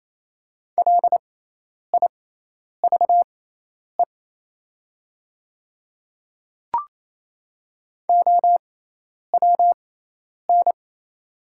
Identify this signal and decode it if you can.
Morse code, heard as a keyed tone, about 28 wpm, spaced as for 10 wpm, 700 Hz. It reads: LIVE OWN
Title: 6x Character Spacing - Top 500 Words - 28wpm